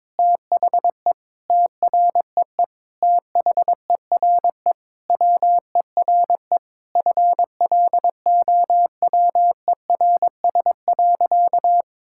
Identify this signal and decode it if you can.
Morse code, heard as a keyed tone, about 22 wpm, 700 Hz. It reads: THE TREE THERE WERE FLOWERS.
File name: Sentences with 9-Word Phrases - 22wpm